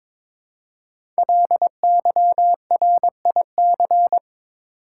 Morse code, heard as a keyed tone, about 22 wpm, 700 Hz. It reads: LYRIC
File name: 5 Letter Words - First 3 Letters Given - 22wpm